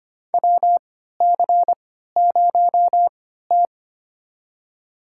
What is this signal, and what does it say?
Morse code, heard as a keyed tone, about 25 wpm, spaced as for 14 wpm, 700 Hz. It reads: WC0T
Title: USA Call Signs - 3x Character Spacing - 25wpm